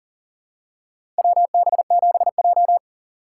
Morse code, heard as a keyed tone, about 40 wpm, 700 Hz. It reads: WB7J
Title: USA Call Signs - 4 Characters - 3x Character Spacing - 40wpm